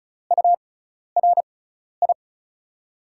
Morse code, heard as a keyed tone, about 36 wpm, spaced as for 12 wpm, 700 Hz. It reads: URI
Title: USA Call Signs - 6x Character Spacing - 35wpm